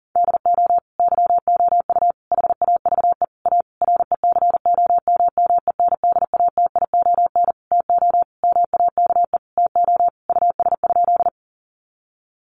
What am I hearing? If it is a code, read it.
Morse code, heard as a keyed tone, about 40 wpm, 700 Hz. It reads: DO YOU HAVE A RECOMMENDATION TO MAKE TO US?
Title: Sentences from Top 1400 Words 40wpm